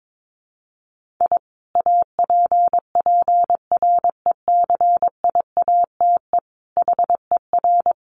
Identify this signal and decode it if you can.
Morse code, heard as a keyed tone, about 22 wpm, 700 Hz. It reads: I APPRECIATE HER
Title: Sentences from Top 1300 Words 22wpm